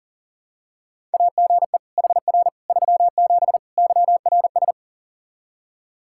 Morse code, heard as a keyed tone, about 40 wpm, 700 Hz. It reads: AGE HR 37 YRS
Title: Practice QSO - Exchange 3 - More Info - 40wpm